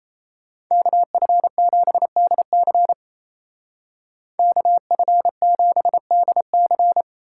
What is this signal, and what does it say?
Morse code, heard as a keyed tone, about 35 wpm, 700 Hz. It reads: KF7DC KF7DC
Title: Speed Racing - USA 2x2 Call Signs 28wpm